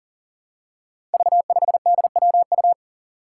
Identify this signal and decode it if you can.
Morse code, heard as a keyed tone, about 41 wpm, 700 Hz. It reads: V5DWU
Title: Five Challenging Letter-Number Combinations 40wpm